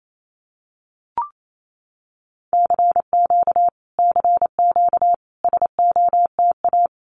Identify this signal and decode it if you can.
Morse code, heard as a keyed tone, about 28 wpm, 700 Hz. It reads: CQ CQ SOTA
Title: Common QSO Phrases Encore 28wpm